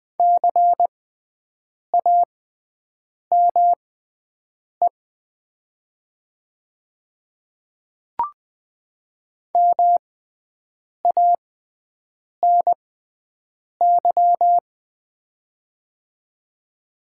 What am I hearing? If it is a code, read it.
Morse code, heard as a keyed tone, about 20 wpm, spaced as for 7 wpm, 700 Hz. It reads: CAME MANY
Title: 6x Character Spacing - Top 200 Words - 20wpm